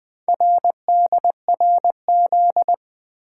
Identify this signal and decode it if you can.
Morse code, heard as a keyed tone, about 20 wpm, 700 Hz. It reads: RDRZ
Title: Four Letter-Number Combinations - 20wpm